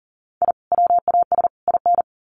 Morse code, heard as a keyed tone, about 40 wpm, 700 Hz. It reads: I WAS IN
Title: Sets of 3 Words 40wpm